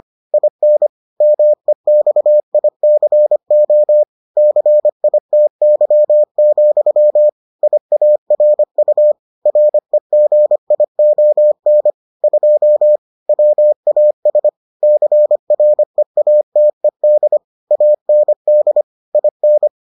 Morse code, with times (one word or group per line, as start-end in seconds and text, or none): 0.3-0.9 IN
1.2-4.0 MEXICO
4.4-7.3 CITY,
7.6-9.1 IARU
9.5-11.9 REGION
12.2-13.0 2
13.3-14.5 WAS
14.8-17.4 CREATED
17.7-18.8 AND
19.1-19.7 IN